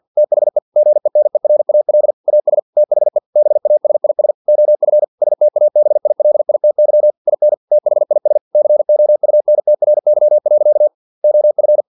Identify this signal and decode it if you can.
Morse code, heard as a keyed tone, about 50 wpm, 600 Hz. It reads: THE GENERAL AS THE BASIS OF STABILITY IN THIS COUNTRY. OF
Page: Never 49wpm